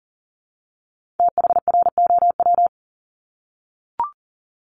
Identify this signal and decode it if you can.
Morse code, heard as a keyed tone, about 38 wpm, 690 Hz. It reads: THROW
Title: Multi-letter Phonemes 40wpm